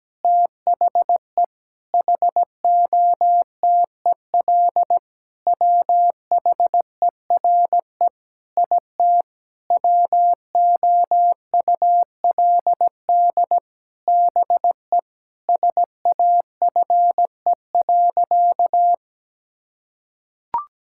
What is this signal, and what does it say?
Morse code, heard as a keyed tone, about 17 wpm, 700 Hz. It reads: THE HOTEL WHERE IT WOULD BE SAFE.
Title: Sentences from Top 1600 Words 17wpm